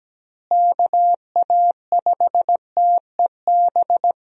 Morse code, heard as a keyed tone, about 17 wpm, 700 Hz. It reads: KA5TEB
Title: USA Call Signs - 6 Characters - Repeated 2x - 17wpm